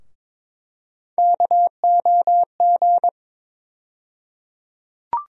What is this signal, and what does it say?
Morse code, heard as a keyed tone, about 21 wpm, 700 Hz. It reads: KOG
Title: USA Call Sign Postfixes - 3 Letters 22wpm